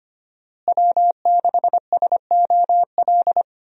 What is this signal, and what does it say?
Morse code, heard as a keyed tone, about 25 wpm, 700 Hz. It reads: W6SOL
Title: USA Call Signs - 5 Characters - Repeated 3x - 25wpm